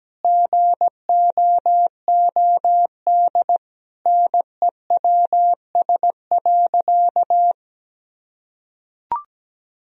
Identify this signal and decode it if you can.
Morse code, heard as a keyed tone, about 17 wpm, 700 Hz. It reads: GOOD NEWS.
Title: Sentences from Top 1000 Words - Review Part 3 - 17wpm